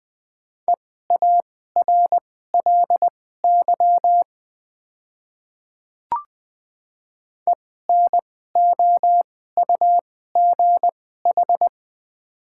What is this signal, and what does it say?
Morse code, heard as a keyed tone, about 20 wpm, spaced as for 14 wpm, 700 Hz. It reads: EARLY ENOUGH
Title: 2x Character Spacing - Top 500 Words - 20wpm